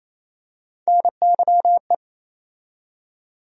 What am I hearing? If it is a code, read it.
Morse code, heard as a keyed tone, about 28 wpm, 700 Hz. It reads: NYE
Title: USA Call Sign Postfixes - 3 Letters 28wpm